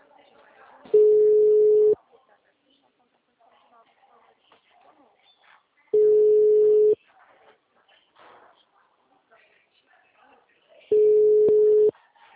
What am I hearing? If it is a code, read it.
Ringback tone (Europe, 425 Hz), with speech.
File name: out- -2005- - - .402